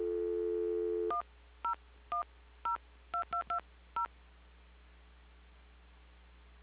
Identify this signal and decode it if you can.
DTMF, heard as telephone touch tones, tones of about 120 ms. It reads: 10102220
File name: recordings / PortlandUnion Millennium